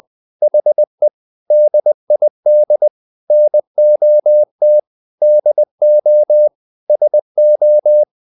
Morse code, heard as a keyed tone, about 20 wpm, 600 Hz. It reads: HE DID NOT DO SO
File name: the old man and the sea part 2 20wpm